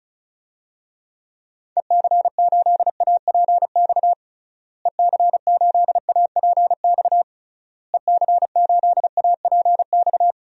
Morse code, heard as a keyed tone, about 36 wpm, 700 Hz. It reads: EC8APX EC8APX EC8APX
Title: DXCC-100 Call Signs - 6 Characters - Repeated 3x - 35wpm